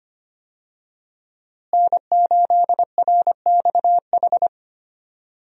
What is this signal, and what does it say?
Morse code, heard as a keyed tone, about 25 wpm, 700 Hz. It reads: N8RXH
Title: USA Call Signs - 5 Characters - Repeated 3x - 25wpm